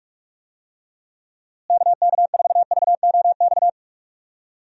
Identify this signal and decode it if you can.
Morse code, heard as a keyed tone, about 46 wpm, 700 Hz. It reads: KK4VOX
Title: USA Call Signs - 6 Characters - Repeated 2x - 45wpm